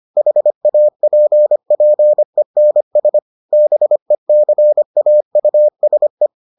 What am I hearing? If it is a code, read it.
Morse code, heard as a keyed tone, about 25 wpm, 600 Hz. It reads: HAPPENS BECAUSE